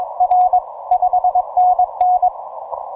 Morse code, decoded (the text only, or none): R 5NN